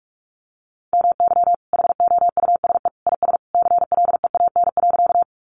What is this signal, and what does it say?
Morse code, heard as a keyed tone, about 44 wpm, 700 Hz. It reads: MY HOUSE IS CLEAN.
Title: Sentences from Top 1100 Words 45wpm